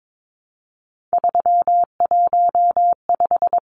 Morse code, heard as a keyed tone, about 22 wpm, 700 Hz. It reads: 315
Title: Numbers - 3-Digit 22wpm